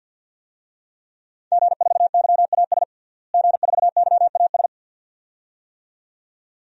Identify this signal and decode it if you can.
Morse code, heard as a keyed tone, about 52 wpm, 700 Hz. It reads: G4YAS G4YAS
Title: DXCC-100 Call Signs - 5 Characters - Repeated 2x - 50wpm